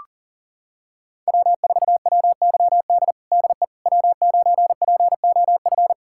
Morse code, heard as a keyed tone, about 40 wpm, 700 Hz. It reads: W4WYD DE W9POF
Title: USA Callsign DE Callsign - 5 Characters - 40wpm